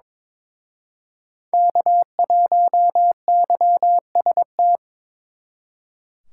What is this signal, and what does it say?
Morse code, heard as a keyed tone, about 26 wpm, 700 Hz. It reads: K1YST
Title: Speed Racing - USA Call Signs 22wpm